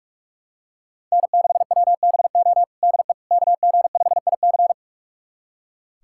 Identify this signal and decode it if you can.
Morse code, heard as a keyed tone, about 43 wpm, 700 Hz. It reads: N6WBO DE KG5IC
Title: USA Callsign DE Callsign - 5 Characters - 45wpm